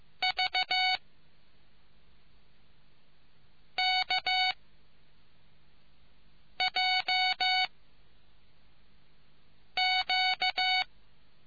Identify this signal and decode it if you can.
Morse code, heard as a keyed tone, about 15 wpm, 2250 Hz. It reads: V K J Q